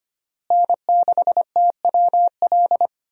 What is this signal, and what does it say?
Morse code, heard as a keyed tone, about 25 wpm, 700 Hz. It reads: N6TWL
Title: USA Call Signs - 5 Characters - Repeated 3x - 25wpm